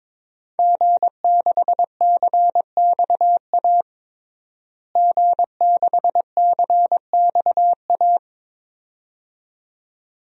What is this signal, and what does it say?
Morse code, heard as a keyed tone, about 22 wpm, 700 Hz. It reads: G6CXA G6CXA
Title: DXCC-100 Call Signs - 5 Characters - Repeated 2x - 22wpm